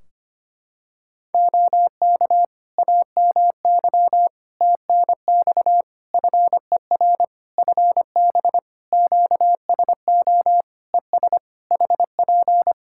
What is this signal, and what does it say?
Morse code, heard as a keyed tone, about 25 wpm, 700 Hz. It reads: OK AMY TNX FER FB QSO ES HP
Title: Practice QSO - Exchange 4 - Ending - 25wpm